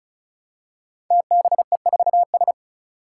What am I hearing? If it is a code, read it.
Morse code, heard as a keyed tone, about 36 wpm, 700 Hz. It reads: TBE4S
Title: Five Challenging Letter-Number Combinations 35wpm